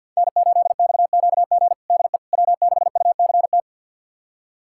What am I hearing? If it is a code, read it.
Morse code, heard as a keyed tone, about 50 wpm, 700 Hz. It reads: N8XQG DE W6UCT
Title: USA Callsign DE Callsign - 5 Characters - 50wpm